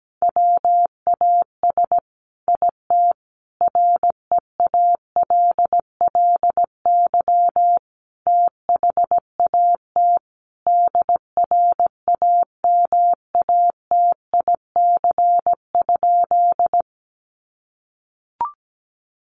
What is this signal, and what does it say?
Morse code, heard as a keyed tone, about 17 wpm, 700 Hz. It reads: WAS IT REALLY THAT DRAMATIC?
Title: Sentences from Top 1300 Words 17wpm